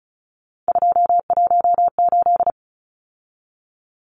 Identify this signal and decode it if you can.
Morse code, heard as a keyed tone, about 34 wpm, 700 Hz. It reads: 218
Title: Numbers - 3-Digit 35wpm